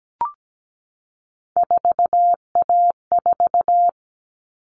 Morse code, heard as a keyed tone, about 17 wpm, 700 Hz. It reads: 4A4
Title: All Cut Numbers - 3-Digit 17wpm